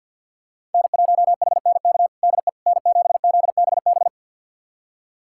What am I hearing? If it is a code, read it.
Morse code, heard as a keyed tone, about 50 wpm, 700 Hz. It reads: N1HNK DE N7ZBB
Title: USA Callsign DE Callsign - 5 Characters - 50wpm